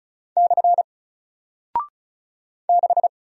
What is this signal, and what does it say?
Morse code, heard as a keyed tone, about 36 wpm, 700 Hz. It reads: / 6
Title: Single Character (Letter:Number:Punctuation:Prosign) Alt 35wpm